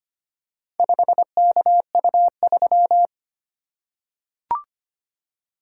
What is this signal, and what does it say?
Morse code, heard as a keyed tone, about 25 wpm, 700 Hz. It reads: 5KU3
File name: practice / 25wpm / Four Letter-Number Combinations - 25wpm